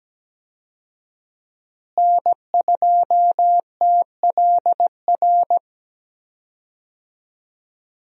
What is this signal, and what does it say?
Morse code, heard as a keyed tone, about 17 wpm, 700 Hz. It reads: N2TLR
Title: USA Call Signs - 5 Characters - Repeated 2x - 17wpm